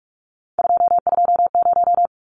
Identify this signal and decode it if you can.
Morse code, heard as a keyed tone, about 44 wpm, 700 Hz. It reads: 220
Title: Numbers - 3-Digit 45wpm